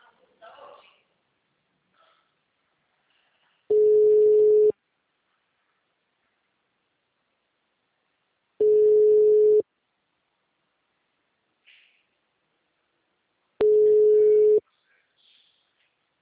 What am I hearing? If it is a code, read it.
Ringback tone (Europe, 425 Hz), with speech.